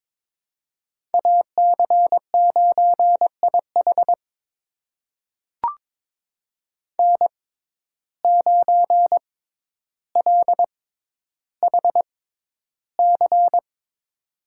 Morse code, heard as a keyed tone, about 23 wpm, spaced as for 11 wpm, 700 Hz. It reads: AC9IH N9LHC
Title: USA Call Signs - 6x Character Spacing - 22wpm